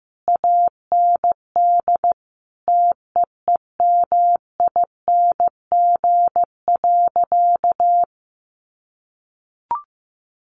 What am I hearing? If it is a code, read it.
Morse code, heard as a keyed tone, about 15 wpm, 700 Hz. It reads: AND TEEMING.
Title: A Sound Of Thunder 15wpm